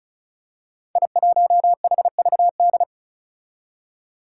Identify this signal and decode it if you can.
Morse code, heard as a keyed tone, about 34 wpm, 700 Hz. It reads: I1HVD
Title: Five Challenging Letter-Number Combinations 35wpm